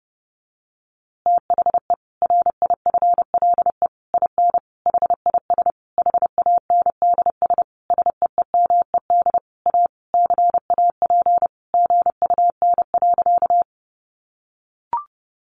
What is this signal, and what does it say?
Morse code, heard as a keyed tone, about 30 wpm, 700 Hz. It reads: THE RIFLE IN HIS HANDS SEEMED A CAP GUN.
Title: A Sound Of Thunder 30wpm